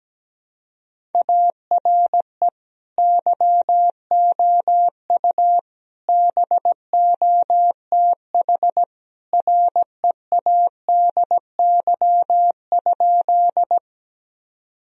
Morse code, heard as a keyed tone, about 17 wpm, 700 Hz. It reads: ARE YOU BOTH READY?
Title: Sentences from Top 1000 Words - Review Part 1 - 17wpm